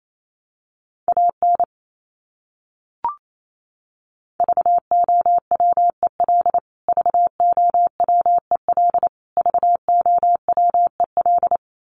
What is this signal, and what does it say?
Morse code, heard as a keyed tone, about 28 wpm, 700 Hz. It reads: AN VOWEL VOWEL VOWEL
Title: Top 500 Words Repeated 28wpm